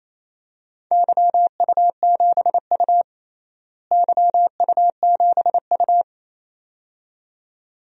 Morse code, heard as a keyed tone, about 28 wpm, 700 Hz. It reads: YU7U YU7U
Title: DXCC-100 Call Signs - 4 Characters - Repeated 2x - 28wpm